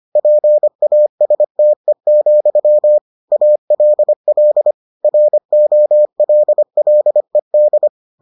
Morse code, heard as a keyed tone, about 25 wpm, 600 Hz. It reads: PASTE, ALL ROLLED